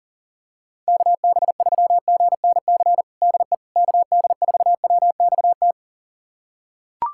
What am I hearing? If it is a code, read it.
Morse code, heard as a keyed tone, about 40 wpm, 700 Hz. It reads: KB3GNC DE KD4WXT
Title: USA Callsign DE Callsign - 6 Characters - 40wpm